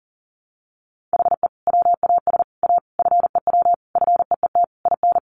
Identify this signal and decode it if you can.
Morse code, heard as a keyed tone, about 40 wpm, 700 Hz. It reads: HE WAS A FEW FEET IN
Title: Sentences from Top 1400 Words 40wpm